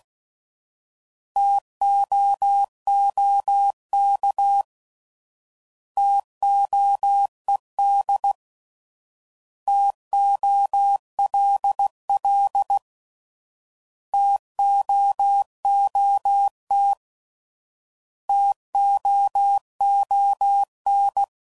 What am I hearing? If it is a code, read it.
Morse code, heard as a keyed tone, about 16 wpm, 800 Hz. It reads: TOOK TOED TOLL TOOT TOON
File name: Prefixes / By First Letter / Ts